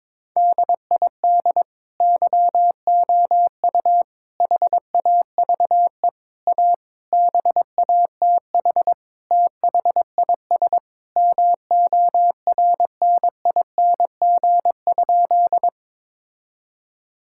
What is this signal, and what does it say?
Morse code, heard as a keyed tone, about 22 wpm, 700 Hz. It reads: DID YOU HAVE A BATH THIS MORNING?
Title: Sentences from Top 1600 Words 22wpm